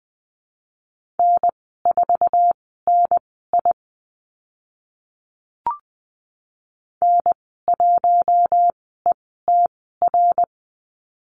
Morse code, heard as a keyed tone, about 20 wpm, spaced as for 14 wpm, 700 Hz. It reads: N4NI N1ETR
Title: 088 Lesson 15g - Number 5 - Call Signs 20wpm